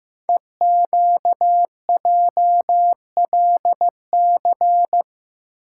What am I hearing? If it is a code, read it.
Morse code, heard as a keyed tone, about 15 wpm, 700 Hz. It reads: EQJLC
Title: Code Groups - 5 Letter:Numbers - 15wpm